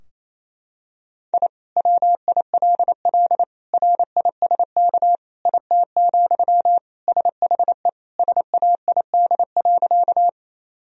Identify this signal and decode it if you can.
Morse code, heard as a keyed tone, about 27 wpm, 700 Hz. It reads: I WILL RISK IT, SHE SAID.
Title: Sentences from Top 1000 Words - Review Part 4 - 28wpm